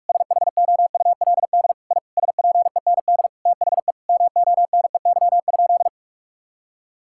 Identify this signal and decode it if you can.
Morse code, heard as a keyed tone, about 41 wpm, 700 Hz. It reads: SHOULD I SPEND THE MONEY?
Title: Sentences from Top 1000 Words - Review Part 5 - 45wpm